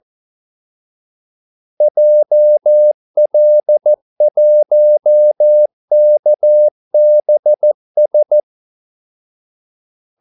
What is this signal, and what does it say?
Morse code, heard as a keyed tone, about 14 wpm, 600 Hz. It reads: JL1KBS
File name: Callsigns-14wpm